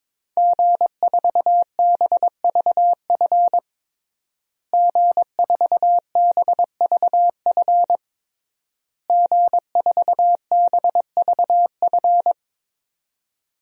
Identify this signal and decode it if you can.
Morse code, heard as a keyed tone, about 22 wpm, 700 Hz. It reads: G4BVF G4BVF G4BVF
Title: DXCC-100 Call Signs - 5 Characters - Repeated 3x - 22wpm